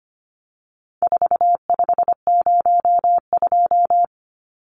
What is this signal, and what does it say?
Morse code, heard as a keyed tone, about 25 wpm, 700 Hz. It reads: 4502
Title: Numbers - 4-Digit 25wpm